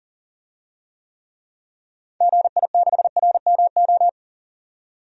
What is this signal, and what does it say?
Morse code, heard as a keyed tone, about 40 wpm, 700 Hz. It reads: GI6RMO
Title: DXCC-100 Call Signs - 6 Characters - 40wpm